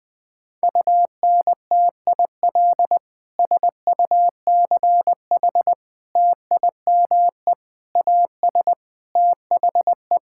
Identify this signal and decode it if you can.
Morse code, heard as a keyed tone, about 19 wpm, 700 Hz. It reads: UNTIL SUCH TIME AS THE
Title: Sentences with 9-Word Phrases - 20wpm